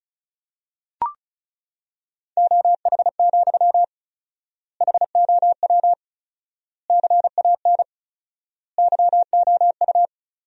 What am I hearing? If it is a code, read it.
Morse code, heard as a keyed tone, about 36 wpm, 700 Hz. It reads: OH, HOW CAN YOU
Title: 4x Word-Spacing - Sentences from Top 500 Words - 35wpm